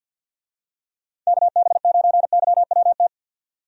Morse code, heard as a keyed tone, about 47 wpm, 700 Hz. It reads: KB9YWT
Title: USA Call Signs - 6 Characters - 50wpm